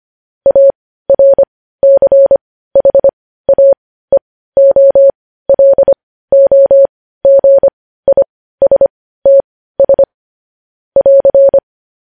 Morse code, heard as a keyed tone, about 25 wpm, spaced as for 15 wpm, 550 Hz. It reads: ARCHAEOLOGISTS <AR>
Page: news-headlines-in-morse-code-at-15-wpm